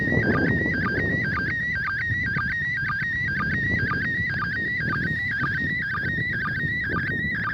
SSTV (slow-scan television), PD 120.
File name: 08-ARISS-SSTV-Expedition-63-ARISS-Series-16-Satellites